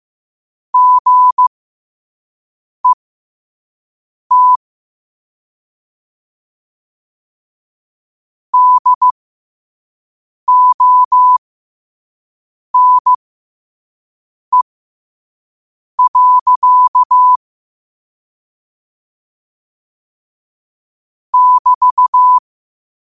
Morse code, heard as a keyed tone, about 15 wpm, spaced as for 5 wpm, 1000 Hz. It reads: GET DONE. <BT>
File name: qotdfiles / 5wpm